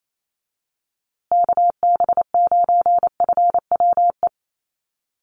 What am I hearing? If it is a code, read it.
Morse code, heard as a keyed tone, about 28 wpm, 700 Hz. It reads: KB9FWE
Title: USA Call Signs Repeated 2x 28wpm